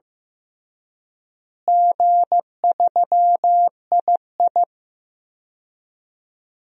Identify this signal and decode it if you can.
Morse code, heard as a keyed tone, about 15 wpm, 700 Hz. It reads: G3II
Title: DXCC-100 Call Signs - 4 Characters - Repeated 2x - 15wpm